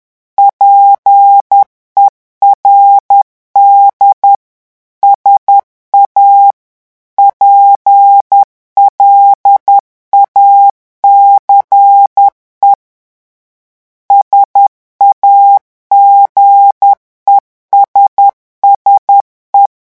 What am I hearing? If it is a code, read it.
Morse code, heard as a keyed tone, about 10 wpm, 800 Hz. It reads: PERD SA PLACE SAGESSE
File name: morse mp3 / revision francais 10 mots minute